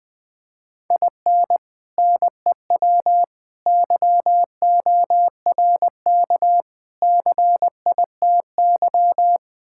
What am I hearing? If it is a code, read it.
Morse code, heard as a keyed tone, about 20 wpm, 700 Hz. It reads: IN NEW YORK CITY
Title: Build A Phrase - 4-Word Phrases - 20wpm